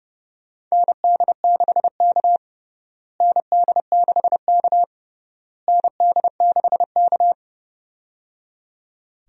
Morse code, heard as a keyed tone, about 30 wpm, 700 Hz. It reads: ND6K ND6K ND6K
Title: USA Call Signs - 4 Characters - Repeated 3x - 30wpm